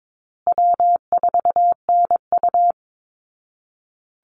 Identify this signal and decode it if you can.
Morse code, heard as a keyed tone, about 22 wpm, 700 Hz. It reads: W4NU
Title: Easy Call Signs Repeated 3x - 22wpm